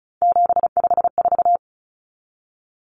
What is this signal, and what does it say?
Morse code, heard as a keyed tone, about 34 wpm, 700 Hz. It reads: 754
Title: Common Cut Numbers - 3-Digit 35wpm